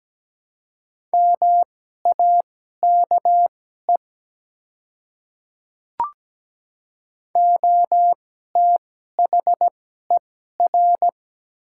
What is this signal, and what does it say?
Morse code, heard as a keyed tone, about 17 wpm, spaced as for 12 wpm, 700 Hz. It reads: MAKE OTHER
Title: 2x Character Spacing - Top 100 Words - 17wpm